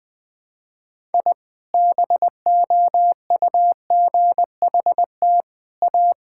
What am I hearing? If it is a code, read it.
Morse code, heard as a keyed tone, about 20 wpm, 700 Hz. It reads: I BOUGHT A
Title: Sentences with 3-Word Phrases - 20wpm